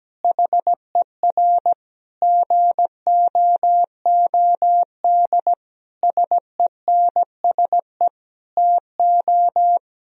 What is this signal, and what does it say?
Morse code, heard as a keyed tone, about 17 wpm, 700 Hz. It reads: HER GOOD SENSE TO
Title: Sentences from Top 1000 Words - Review Part 4 - 17wpm